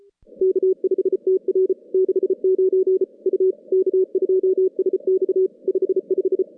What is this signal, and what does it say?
Morse code, heard as a keyed tone, about 34 wpm, 380 Hz. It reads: K5TR 69 U K2SX 55
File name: contests / CW-SS-k5tr- -2100